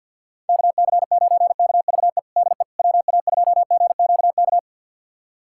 Morse code, heard as a keyed tone, about 50 wpm, 700 Hz. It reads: KC9KVE DE WA2GQK
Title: USA Callsign DE Callsign - 6 Characters - 50wpm